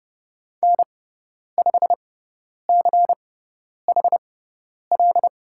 Morse code, heard as a keyed tone, about 30 wpm, spaced as for 10 wpm, 700 Hz. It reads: N5CHL
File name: Lesson 16g - Letter c - Call Signs 30wpm